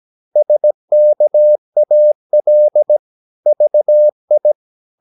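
Morse code, heard as a keyed tone, about 17 wpm, 600 Hz. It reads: SKAL VI
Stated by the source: Slow (QRS) Morse practice at 17 wpm, from CW med Gnister.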